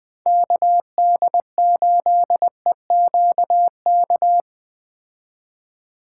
Morse code, heard as a keyed tone, about 20 wpm, 700 Hz. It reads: KD8EQK